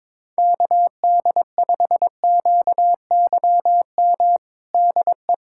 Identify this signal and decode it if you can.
Morse code, heard as a keyed tone, about 22 wpm, 700 Hz. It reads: KD5QYM DE
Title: USA Callsign DE Callsign - 6 Characters - 22wpm